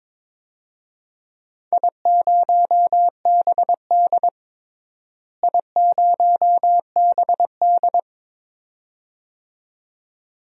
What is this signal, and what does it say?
Morse code, heard as a keyed tone, about 22 wpm, 700 Hz. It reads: I0BD I0BD
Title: DXCC-100 Call Signs - 4 Characters - Repeated 2x - 22wpm